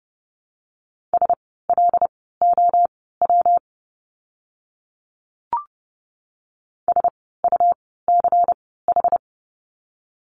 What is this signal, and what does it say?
Morse code, heard as a keyed tone, about 30 wpm, spaced as for 17 wpm, 700 Hz. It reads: SLOW SUCH
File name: course / 30:10wpm / 115 Lesson 19f - Letter w - Words 30wpm